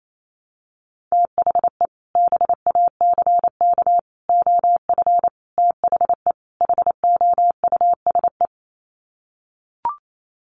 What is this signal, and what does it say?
Morse code, heard as a keyed tone, about 28 wpm, 700 Hz. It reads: THE BACK OF THE HOUSE
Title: Sets of 5 Words Encore2 28wpm